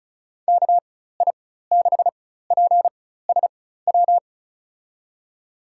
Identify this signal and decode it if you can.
Morse code, heard as a keyed tone, about 36 wpm, spaced as for 16 wpm, 700 Hz. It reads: KI6PSW
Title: USA Call Signs - 4x Character Spacing - 35wpm